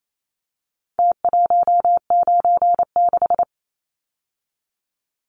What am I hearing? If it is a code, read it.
Morse code, heard as a keyed tone, about 28 wpm, 700 Hz. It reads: T196
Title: Numbers - Contest Style - 4-Digit 28wpm